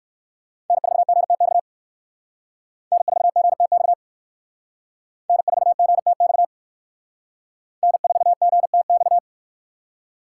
Morse code, heard as a keyed tone, about 57 wpm, 700 Hz. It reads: N4GTX N4GTX N4GTX N4GTX